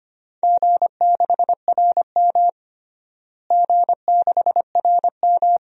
Morse code, heard as a keyed tone, about 25 wpm, 700 Hz. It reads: G6RM G6RM
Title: DXCC-100 Call Signs - 4 Characters - Repeated 3x - 25wpm